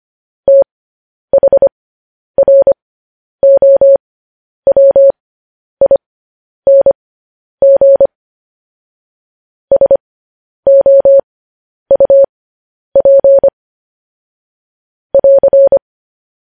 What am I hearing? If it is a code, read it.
Morse code, heard as a keyed tone, about 25 wpm, spaced as for 10 wpm, 550 Hz. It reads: THROWING SOUP <AR>
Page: news-headlines-in-morse-code-at-10-wpm